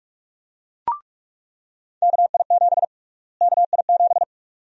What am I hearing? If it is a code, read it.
Morse code, heard as a keyed tone, about 46 wpm, 700 Hz. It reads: KI7 KI7
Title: USA Call Sign Prefixes - Repeated 2x - 45wpm